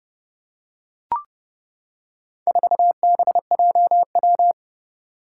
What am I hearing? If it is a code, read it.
Morse code, heard as a keyed tone, about 31 wpm, 700 Hz. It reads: 4BJW
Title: Four Challenging Letter-Number Combinations 30wpm